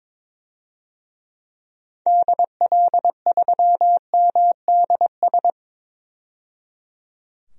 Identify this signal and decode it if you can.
Morse code, heard as a keyed tone, about 21 wpm, 700 Hz. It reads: DL3MDS
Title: DXCC-100 Call Signs - 6 Characters - 22wpm